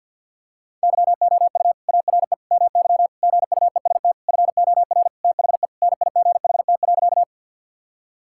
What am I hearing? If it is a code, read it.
Morse code, heard as a keyed tone, about 50 wpm, 700 Hz. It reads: YOU ARE MY GUEST FOR THE NIGHT.
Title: Sentences from Top 1600 Words 50wpm